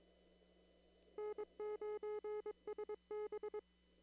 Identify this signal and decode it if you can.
Morse code, heard as a keyed tone, about 22 wpm, 400 Hz. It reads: N9SB
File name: CW / kt5j wpx cw-op-k5pi